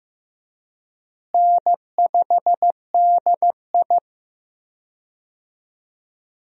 Morse code, heard as a keyed tone, about 15 wpm, 700 Hz. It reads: N5DI
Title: USA Call Signs - 4 Characters - Repeated 2x - 15wpm